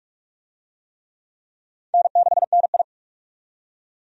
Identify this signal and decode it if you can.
Morse code, heard as a keyed tone, about 46 wpm, 700 Hz. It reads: N6NI
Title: USA Call Signs - 4 Characters - 45wpm